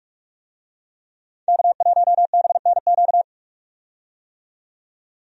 Morse code, heard as a keyed tone, about 44 wpm, 700 Hz. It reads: K1BNQ
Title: USA Call Signs - 5 Characters - 45wpm